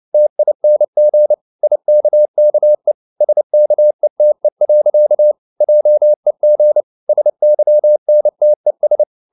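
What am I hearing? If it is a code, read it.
Morse code, heard as a keyed tone, about 29 wpm, 600 Hz. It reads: TING IKKE SKETE. JEG SYNTES
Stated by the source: Fast (QRQ) Morse at 29 wpm, from CW med Gnister.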